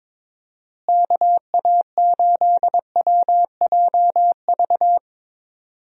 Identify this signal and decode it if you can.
Morse code, heard as a keyed tone, about 21 wpm, 700 Hz. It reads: KA8WJV
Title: USA Call Signs - 6 Characters - Repeated 2x - 22wpm